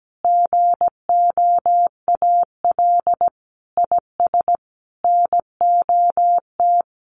Morse code, heard as a keyed tone, about 17 wpm, 700 Hz. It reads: GOAL IS NOT
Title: Sentences from Top 1400 Words 17wpm